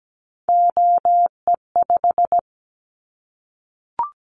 Morse code, heard as a keyed tone, about 17 wpm, 700 Hz. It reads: OE5
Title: Callsign Prefixes for Top 50 DXCC - 17wpm